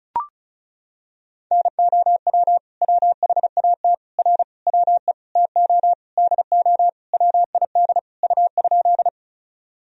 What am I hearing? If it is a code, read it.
Morse code, heard as a keyed tone, about 36 wpm, 700 Hz. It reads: NOW WHAT R WE TO DO WID U?
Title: Sentences from Top 100 Words and Abbreviations 35wpm